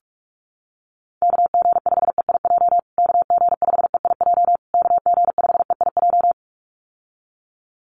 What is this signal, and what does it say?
Morse code, heard as a keyed tone, about 44 wpm, 700 Hz. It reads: KG5EIJ KG5EIJ KG5EIJ
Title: USA Call Signs Repeated 3x 45wpm